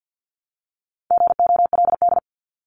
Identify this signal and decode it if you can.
Morse code, heard as a keyed tone, about 50 wpm, 700 Hz. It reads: GOLD
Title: Top 500 Words 50wpm